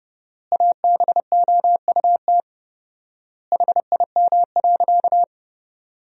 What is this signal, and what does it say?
Morse code, heard as a keyed tone, about 30 wpm, 700 Hz. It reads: ABOUT HIM.
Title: 4x Word-Spacing - Sentences from Top 200 Words - 30wpm